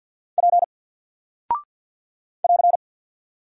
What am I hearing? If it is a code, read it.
Morse code, heard as a keyed tone, about 48 wpm, 700 Hz. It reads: P <AR>
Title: Single Character (Letter:Number:Punctuation:Prosign) 50wpm